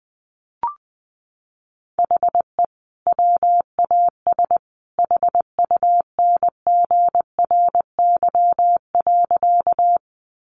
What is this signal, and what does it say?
Morse code, heard as a keyed tone, about 20 wpm, 700 Hz. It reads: HE WAS HUNGRY.
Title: Sentences from Top 1400 Words 20wpm